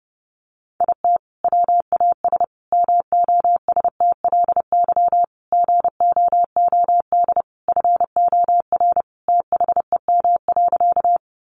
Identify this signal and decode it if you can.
Morse code, heard as a keyed tone, about 30 wpm, 700 Hz. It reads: IT WAS MOSTLY GOOD FOR THEM.
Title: Sentences from Top 1300 Words 30wpm